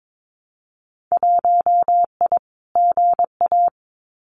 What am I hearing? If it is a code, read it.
Morse code, heard as a keyed tone, about 22 wpm, 700 Hz. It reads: 1I GA
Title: Winter Field Day Exchanges Encore 22wpm